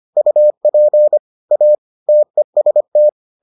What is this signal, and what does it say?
Morse code, heard as a keyed tone, about 25 wpm, 600 Hz. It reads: UP A TEST